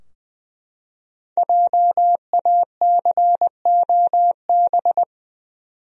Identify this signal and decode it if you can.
Morse code, heard as a keyed tone, about 20 wpm, 700 Hz. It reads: JACOB
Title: USA Common Names - 20wpm